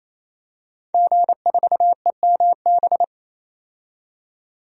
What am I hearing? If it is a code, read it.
Morse code, heard as a keyed tone, about 28 wpm, 700 Hz. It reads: G4EMB
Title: DXCC-100 Call Signs - 5 Characters - 28wpm